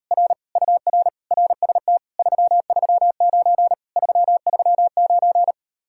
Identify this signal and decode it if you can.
Morse code, heard as a keyed tone, about 38 wpm, 700 Hz. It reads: R UR RST 339 339